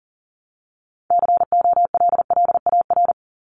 Morse code, heard as a keyed tone, about 39 wpm, 700 Hz. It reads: COLLAR
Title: Top 1751-2000 Words 40wpm